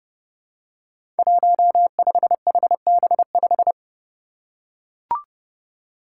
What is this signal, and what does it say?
Morse code, heard as a keyed tone, about 30 wpm, 700 Hz. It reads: 15HB5
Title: Five Challenging Letter-Number Combinations 30wpm